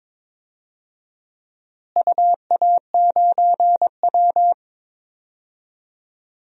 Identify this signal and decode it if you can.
Morse code, heard as a keyed tone, about 22 wpm, 700 Hz. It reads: UA9W
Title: DXCC-100 Call Signs - 4 Characters - Repeated 2x - 22wpm